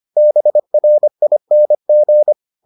Morse code, heard as a keyed tone, about 25 wpm, 600 Hz. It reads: BRING